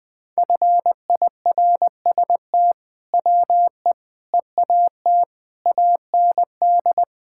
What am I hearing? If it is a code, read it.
Morse code, heard as a keyed tone, about 20 wpm, 700 Hz. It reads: FIRST WE EAT AND
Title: Sentences with 2-Word Phrases - Spoken Beginning - 20wpm